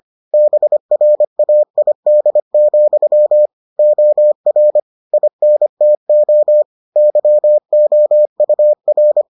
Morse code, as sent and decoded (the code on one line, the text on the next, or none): -... .-. .- .. -.. --..-- / --- .-. / .. -. - --- / -.-- --- ..- .-.
BRAID, OR INTO YOUR